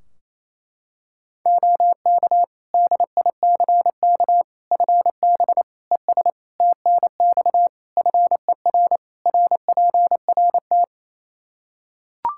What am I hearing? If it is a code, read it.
Morse code, heard as a keyed tone, about 28 wpm, 700 Hz. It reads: OK DICK FB ES TNX FER RPRT
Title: Practice QSO - Exchange 2 - Info - 28wpm